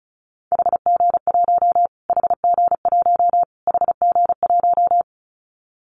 Morse code, heard as a keyed tone, about 34 wpm, 700 Hz. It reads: HG1 HG1 HG1
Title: Callsign Prefixes for Top 25 DXCC - Repeated 3x - 35wpm